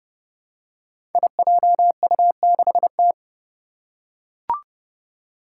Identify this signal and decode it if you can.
Morse code, heard as a keyed tone, about 30 wpm, 700 Hz. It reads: IJU6T
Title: Five Challenging Letter-Number Combinations 30wpm